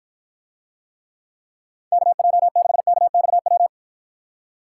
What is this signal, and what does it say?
Morse code, heard as a keyed tone, about 53 wpm, 700 Hz. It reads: KJ6KXW